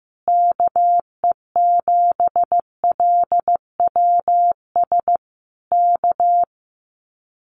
Morse code, heard as a keyed tone, about 16 wpm, 700 Hz. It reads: KE7LWS K
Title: Common QSO Phrases Encore 15wpm